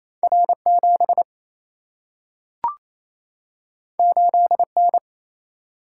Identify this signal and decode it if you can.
Morse code, heard as a keyed tone, about 28 wpm, 700 Hz. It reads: R 7 8 N
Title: Single-Letter Single-Number 28wpm